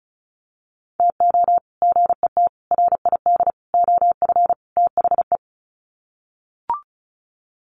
Morse code, heard as a keyed tone, about 34 wpm, 700 Hz. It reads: TO GET RID OF THE
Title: Sets of 5 Words 35wpm